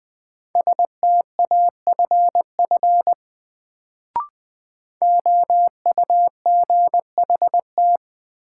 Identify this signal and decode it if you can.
Morse code, heard as a keyed tone, about 20 wpm, 700 Hz. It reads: STAFF OUGHT
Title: Sets of 5 Words Encore - New Word Drill - 20wpm